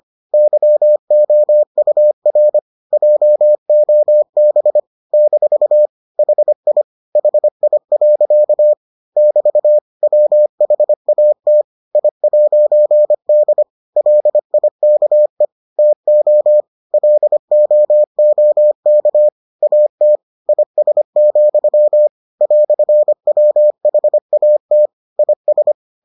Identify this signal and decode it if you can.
Morse code, heard as a keyed tone, about 25 wpm, 600 Hz. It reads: YOUR JOB - HI HI. <BT> WHAT I'D LIKE TO LOOK AT IS, "WHAT IS